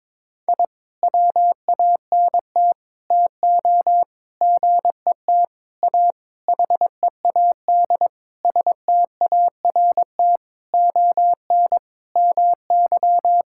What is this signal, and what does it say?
Morse code, heard as a keyed tone, about 21 wpm, 700 Hz. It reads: I WANT TO GET A HEAD START ON MY
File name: Sentences with 7-Word Phrases - 22wpm